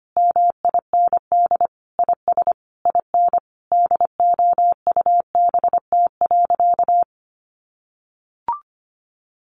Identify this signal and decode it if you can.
Morse code, heard as a keyed tone, about 25 wpm, 700 Hz. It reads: MIND IS IN DOUBT.
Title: Sentences from Top 1500 Words 25wpm